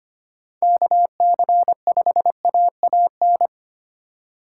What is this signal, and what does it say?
Morse code, heard as a keyed tone, about 29 wpm, 700 Hz. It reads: KC5AAN
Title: Speed Racing - USA 2x3 Call Signs 25wpm